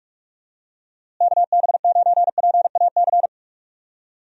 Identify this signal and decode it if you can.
Morse code, heard as a keyed tone, about 46 wpm, 700 Hz. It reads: KB9PAC
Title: USA Call Signs - 6 Characters - Repeated 2x - 45wpm